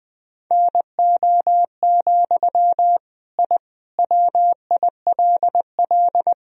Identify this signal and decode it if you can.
Morse code, heard as a keyed tone, about 21 wpm, 700 Hz. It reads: NO, I WILL
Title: Sentences from Top 1000 Words - Review Part 1 - 20wpm